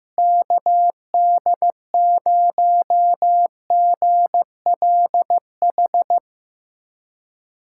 Morse code, heard as a keyed tone, about 15 wpm, 700 Hz. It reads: KD0GLH
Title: USA Callsign DE Callsign 15wpm